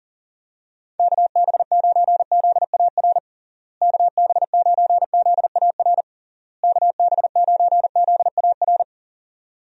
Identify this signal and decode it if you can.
Morse code, heard as a keyed tone, about 37 wpm, 700 Hz. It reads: KB9ZAR KB9ZAR KB9ZAR
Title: USA Call Signs - 6 Characters - Repeated 3x - 40wpm